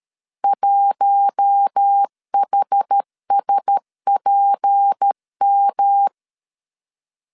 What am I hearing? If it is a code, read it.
Morse code, heard as a keyed tone, about 13 wpm, 800 Hz. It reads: 1HSPM
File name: LEZIONI / 6 PKX0 / 6 P 12wpm 2033